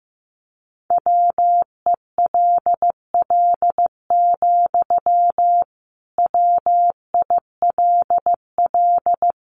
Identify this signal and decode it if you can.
Morse code, heard as a keyed tone, about 15 wpm, 700 Hz. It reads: WELL, WILL
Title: Sentences from Top 1300 Words 15wpm